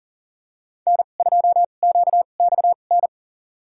Morse code, heard as a keyed tone, about 40 wpm, spaced as for 29 wpm, 700 Hz. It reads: N2QXN
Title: USA Call Signs - 5 Characters - 2x Character Spacing - 40wpm